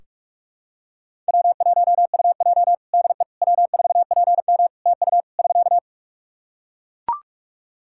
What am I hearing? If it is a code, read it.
Morse code, heard as a keyed tone, about 41 wpm, 700 Hz. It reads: W1UJ DE W4PM TU <SK>
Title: K1USN SST QSO Practice 45wpm